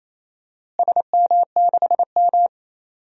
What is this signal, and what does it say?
Morse code, heard as a keyed tone, about 28 wpm, 700 Hz. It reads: SM6M
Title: DXCC-100 Call Signs - 4 Characters - Repeated 2x - 28wpm